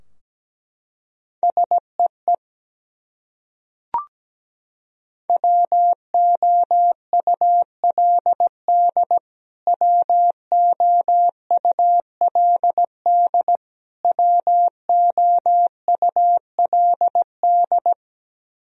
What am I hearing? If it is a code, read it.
Morse code, heard as a keyed tone, about 17 wpm, 700 Hz. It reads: SEE WOULD WOULD WOULD
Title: Top 100 Words Repeated 17wpm